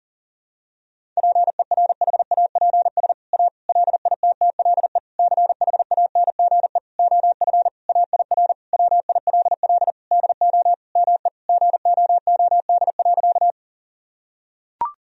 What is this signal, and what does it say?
Morse code, heard as a keyed tone, about 40 wpm, 700 Hz. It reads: PERHAPS A LITTLE CHANGE OF AIR WILL DO ME GOOD.
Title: Sentences from Top 1000 Words - Review Part 1 - 40wpm